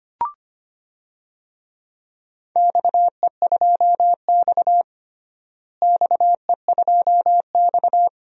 Morse code, heard as a keyed tone, about 25 wpm, 700 Hz. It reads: XE2X XE2X
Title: DXCC-100 Call Signs - 4 Characters - Repeated 2x - 25wpm